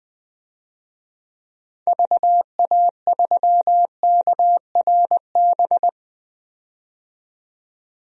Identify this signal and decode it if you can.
Morse code, heard as a keyed tone, about 20 wpm, 700 Hz. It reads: VA3KRB
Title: DXCC-100 Call Signs - 6 Characters - 20wpm